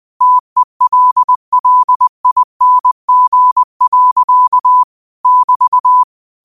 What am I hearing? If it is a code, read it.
Morse code, heard as a keyed tone, about 20 wpm, 1000 Hz. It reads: TELLING. <BT>
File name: qotdfiles / 20wpm